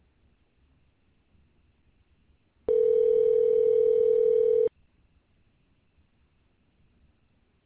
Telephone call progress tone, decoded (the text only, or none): ringback tone